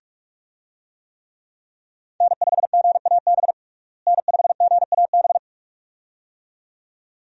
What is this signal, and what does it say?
Morse code, heard as a keyed tone, about 46 wpm, 700 Hz. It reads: N5GAB N5GAB
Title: USA Call Signs - 5 Characters - Repeated 2x - 45wpm